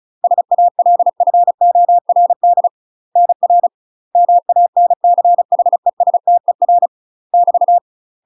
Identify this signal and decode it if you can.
Morse code, heard as a keyed tone, about 36 wpm, 700 Hz. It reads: SALFORD NR MANCHESTER <BT>
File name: UK MASTER QSOs TRAINING / 35WPM / UK MASTER QSOs 0036